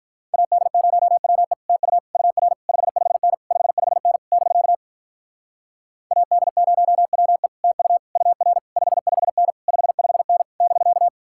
Morse code, heard as a keyed tone, about 48 wpm, 700 Hz. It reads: AD0WE TU UR 55N 55N <BK> AD0WE TU UR 55N 55N <BK>